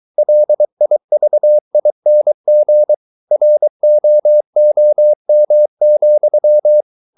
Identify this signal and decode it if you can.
Morse code, heard as a keyed tone, about 23 wpm, 600 Hz. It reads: LIVING ROOM,